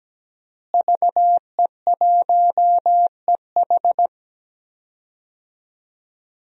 Morse code, heard as a keyed tone, about 17 wpm, 700 Hz. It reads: VE1EH
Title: DXCC-100 Call Signs - 5 Characters - Repeated 2x - 17wpm